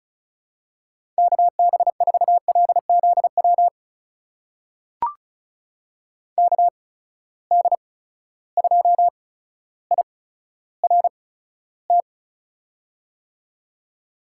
Morse code, heard as a keyed tone, about 36 wpm, spaced as for 15 wpm, 700 Hz. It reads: KB4LZW KD2IRT
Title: USA Call Signs - 6 Characters - 8x Character Spacing - 35wpm